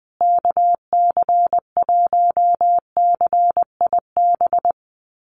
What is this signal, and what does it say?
Morse code, heard as a keyed tone, about 20 wpm, 700 Hz. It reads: KC1CIB
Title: USA Call Signs Repeated 3x 20wpm